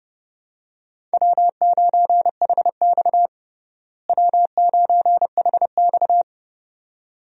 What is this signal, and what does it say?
Morse code, heard as a keyed tone, about 30 wpm, 700 Hz. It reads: W9HX W9HX
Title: USA Call Signs - 4 Characters - Repeated 2x - 30wpm